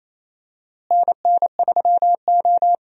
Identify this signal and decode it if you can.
Morse code, heard as a keyed tone, about 27 wpm, 700 Hz. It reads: NN3O
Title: USA Call Signs - 4 Characters - Repeated 3x - 28wpm